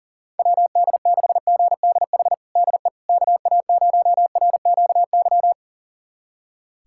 Morse code, heard as a keyed tone, about 37 wpm, 700 Hz. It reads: WD6GDH DE KA0RQY
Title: USA Callsign DE Callsign - 6 Characters - 40wpm